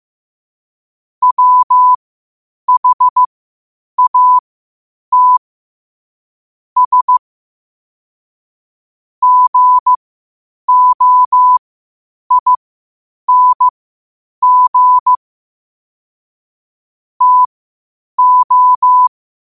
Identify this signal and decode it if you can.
Morse code, heard as a keyed tone, about 15 wpm, spaced as for 8 wpm, 1000 Hz. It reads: WHAT S GOING TO
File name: qotdfiles / 8wpm